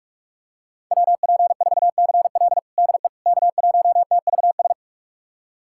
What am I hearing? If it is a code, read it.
Morse code, heard as a keyed tone, about 42 wpm, 700 Hz. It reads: WP4CL DE K1TVS
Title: USA Callsign DE Callsign - 5 Characters - 45wpm